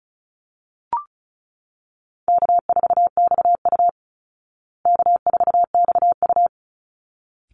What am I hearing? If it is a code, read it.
Morse code, heard as a keyed tone, about 34 wpm, 700 Hz. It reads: K4XU K4XU
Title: Easy Call Signs Repeated 2x - 35wpm